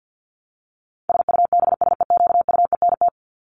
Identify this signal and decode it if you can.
Morse code, heard as a keyed tone, about 50 wpm, 700 Hz. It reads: SUBSEQUENT
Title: LY Suffix - Top 100 Words with Base Words 50wpm